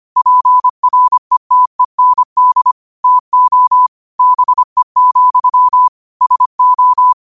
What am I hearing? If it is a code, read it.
Morse code, heard as a keyed tone, about 25 wpm, 1000 Hz. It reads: PRETEND TO BE, SO
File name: qotdfiles / 25wpm